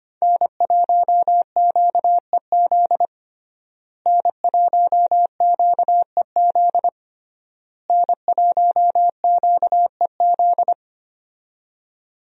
Morse code, heard as a keyed tone, about 25 wpm, 700 Hz. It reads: N1QEZ N1QEZ N1QEZ
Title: USA Call Signs - 5 Characters - Repeated 3x - 25wpm